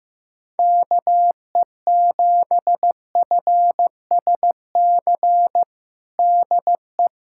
Morse code, heard as a keyed tone, about 15 wpm, 700 Hz. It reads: KE7FSC DE
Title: USA Callsign DE Callsign - 6 Characters - 15wpm